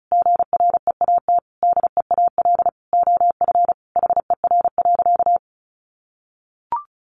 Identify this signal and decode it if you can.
Morse code, heard as a keyed tone, about 34 wpm, 700 Hz. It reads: GREAT DEAL OF HER.
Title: Sentences from Top 800 Words 35wpm